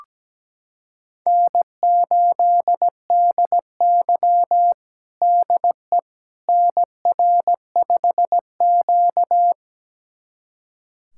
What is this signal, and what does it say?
Morse code, heard as a keyed tone, about 17 wpm, 700 Hz. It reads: N8DY DE NR5Q
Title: USA Callsign DE Callsign - 4 Characters - 17wpm